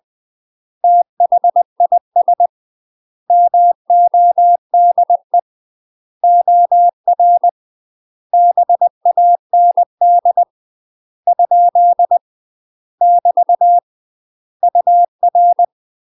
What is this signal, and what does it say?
Morse code, heard as a keyed tone, about 20 wpm, 700 Hz. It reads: THIS MODE OR BAND ? <BT> UR
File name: UK MASTER QSOs TRAINING / 20WPM / UK MASTER QSOs 0028